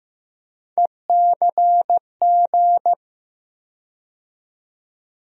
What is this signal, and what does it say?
Morse code, heard as a keyed tone, about 15 wpm, 700 Hz. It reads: ECG
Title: USA Call Sign Postfixes - 3 Letters 15wpm